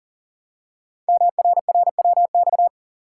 Morse code, heard as a keyed tone, about 38 wpm, 690 Hz. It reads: MRRWX
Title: Code Groups - 5 Letters - 40wpm